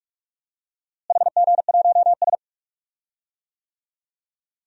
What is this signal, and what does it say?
Morse code, heard as a keyed tone, about 43 wpm, 700 Hz. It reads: HG1S
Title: DXCC-100 Call Signs - 4 Characters - Repeated 3x - 45wpm